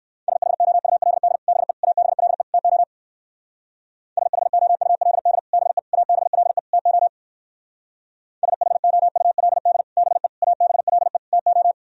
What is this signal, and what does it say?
Morse code, heard as a keyed tone, about 62 wpm, 700 Hz. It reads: SHOULD BE ABLE TO SHOULD BE ABLE TO SHOULD BE ABLE TO